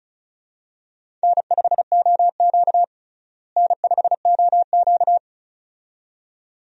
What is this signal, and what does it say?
Morse code, heard as a keyed tone, about 36 wpm, 700 Hz. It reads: N5OQ N5OQ
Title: USA Call Signs - 4 Characters - Repeated 2x - 35wpm